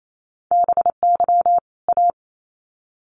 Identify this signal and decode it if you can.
Morse code, heard as a keyed tone, about 28 wpm, 700 Hz. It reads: BY A
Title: Sets of 2 Words 28wpm